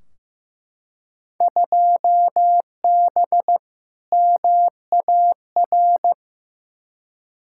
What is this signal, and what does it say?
Morse code, heard as a keyed tone, about 15 wpm, 700 Hz. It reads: 2B MAR
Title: Field Day Exchanges Encore - 15wpm